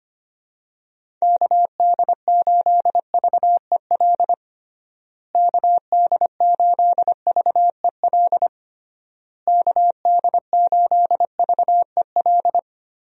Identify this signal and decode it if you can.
Morse code, heard as a keyed tone, about 25 wpm, 700 Hz. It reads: KD8VEL KD8VEL KD8VEL
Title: USA Call Signs - 6 Characters - Repeated 3x - 25wpm